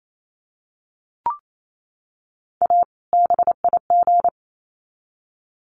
Morse code, heard as a keyed tone, about 28 wpm, 700 Hz. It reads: A BIG
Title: Sets of 2 Words 28wpm